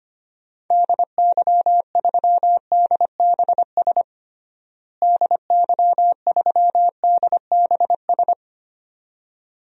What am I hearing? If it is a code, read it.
Morse code, heard as a keyed tone, about 25 wpm, 700 Hz. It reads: DY3DBS DY3DBS
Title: DXCC-100 Call Signs - 6 Characters - Repeated 2x - 25wpm